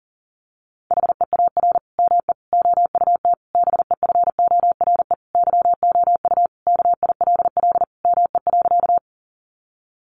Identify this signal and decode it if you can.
Morse code, heard as a keyed tone, about 40 wpm, 700 Hz. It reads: HEAR ME OUT BEFORE YOU KILL ME.
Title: Sentences from Top 1400 Words 40wpm